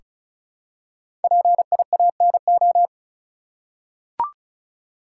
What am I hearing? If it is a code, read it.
Morse code, heard as a keyed tone, about 36 wpm, 700 Hz. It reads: PIANO
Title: Five Letter Words - 35wpm